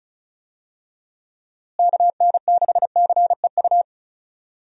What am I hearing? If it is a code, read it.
Morse code, heard as a keyed tone, about 36 wpm, 700 Hz. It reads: KN6CEU
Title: USA Call Signs - 6 Characters - Repeated 2x - 35wpm